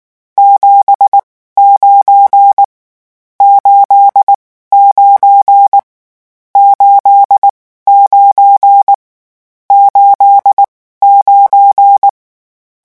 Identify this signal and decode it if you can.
Morse code, heard as a keyed tone, about 19 wpm, spaced as for 14 wpm, 800 Hz. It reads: 79 89 89 89
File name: morse mp3 / lecon12-vitesse 18